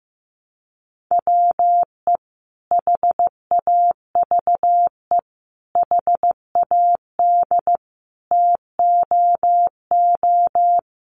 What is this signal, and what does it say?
Morse code, heard as a keyed tone, about 14 wpm, 700 Hz. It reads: WE HAVE HAD TOO
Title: Sentences from Top 700 Words 15wpm